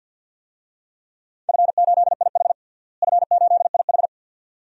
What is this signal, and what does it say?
Morse code, heard as a keyed tone, about 50 wpm, 700 Hz. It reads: F8IH F8IH
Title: DXCC-100 Call Signs - 4 Characters - Repeated 2x - 50wpm